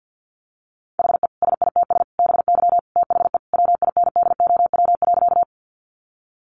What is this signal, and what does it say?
Morse code, heard as a keyed tone, about 50 wpm, 700 Hz. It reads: HE SITS BY THE WINDOW.
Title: Easy Phrases 50wpm